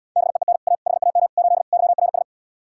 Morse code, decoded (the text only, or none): BEEN A HAM 8 YRS